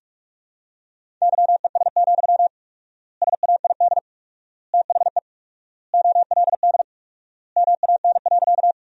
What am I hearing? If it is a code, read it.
Morse code, heard as a keyed tone, about 46 wpm, 700 Hz. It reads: YES, SAID THE OLD MAN.
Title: 4x Word-Spacing - Sentences from Top 500 Words - 45wpm